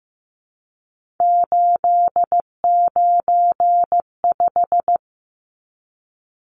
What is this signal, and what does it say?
Morse code, heard as a keyed tone, about 15 wpm, 700 Hz. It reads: 895
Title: Numbers - 3-Digit 15wpm